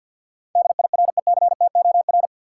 Morse code, heard as a keyed tone, about 49 wpm, 700 Hz. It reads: DIRECTOR
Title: Sentences from Top 1300 Words - New Word Drill - 50wpm